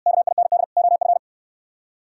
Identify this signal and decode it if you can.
Morse code, heard as a keyed tone, about 67 wpm, 700 Hz. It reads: KIND OF